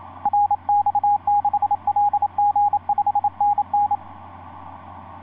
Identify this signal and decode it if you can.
Morse code, heard as a keyed tone, about 27 wpm, 820 Hz, with radio noise. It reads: RX6LG 5NN
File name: 9x0lx sp1mwk etc 12m cw 12feb2010 1146